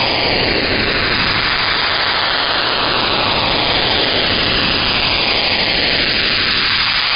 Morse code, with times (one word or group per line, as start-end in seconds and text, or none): none